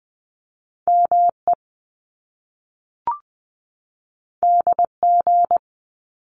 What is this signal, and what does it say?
Morse code, heard as a keyed tone, about 20 wpm, 700 Hz. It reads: ME DG
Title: Top Two Letter Combinations in Words 20wpm